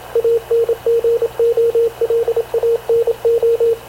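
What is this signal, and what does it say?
Morse code, heard as a keyed tone, about 27 wpm, 460 Hz, with radio noise. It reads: ANGOLANO
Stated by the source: Havana Radio / CLA sending Spanish press (PX) in Morse, recorded by Radio Officer Paul Dumesnil onboard CNAV QUEST/CZDO.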